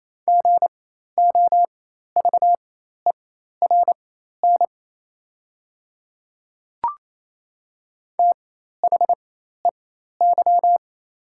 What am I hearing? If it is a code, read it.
Morse code, heard as a keyed tone, about 28 wpm, spaced as for 13 wpm, 700 Hz. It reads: GOVERN THEY
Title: 4x Character Spacing - Top 500 Words - 28wpm